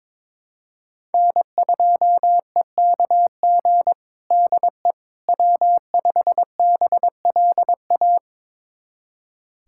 Morse code, heard as a keyed tone, about 21 wpm, 700 Hz. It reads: N2EKG DE W5BLA
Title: USA Callsign DE Callsign - 5 Characters - 22wpm